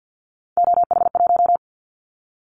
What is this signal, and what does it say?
Morse code, heard as a keyed tone, about 50 wpm, 700 Hz. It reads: OH1
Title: Callsign Prefixes for Top 50 DXCC - 50wpm